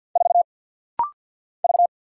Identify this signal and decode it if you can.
Morse code, heard as a keyed tone, about 50 wpm, 700 Hz. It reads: 4 V
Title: Single Letter-Number Alt - Rapid-Fire 50wpm